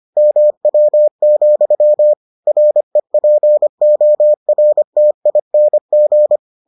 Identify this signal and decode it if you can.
Morse code, heard as a keyed tone, about 25 wpm, 600 Hz. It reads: MW, REPORTING